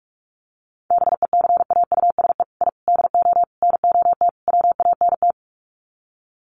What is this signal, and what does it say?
Morse code, heard as a keyed tone, about 44 wpm, 700 Hz. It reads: BECAUSE I DO NOT WANT
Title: Sets of 5 Words Encore 45wpm